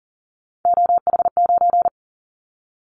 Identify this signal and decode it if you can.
Morse code, heard as a keyed tone, about 40 wpm, 700 Hz. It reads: OH9
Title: Callsign Prefixes for Top 25 DXCC - 40wpm